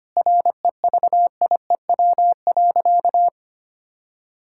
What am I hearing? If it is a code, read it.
Morse code, heard as a keyed tone, about 25 wpm, 700 Hz. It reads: REVIEW.
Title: Sentences from Top 1000 Words - Review Part 6 - 25wpm